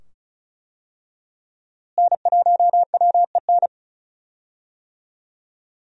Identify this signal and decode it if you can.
Morse code, heard as a keyed tone, about 36 wpm, 700 Hz. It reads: N1WEN
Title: USA Call Signs - 5 Characters - Repeated 2x - 35wpm